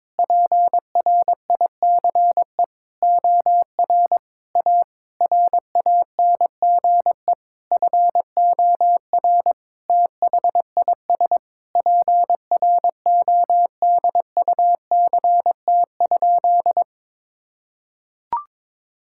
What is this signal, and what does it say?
Morse code, heard as a keyed tone, about 22 wpm, 700 Hz. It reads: PRICE OR A RANGE FOR THIS PRODUCT?
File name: Sentences with 9-Word Phrases - 22wpm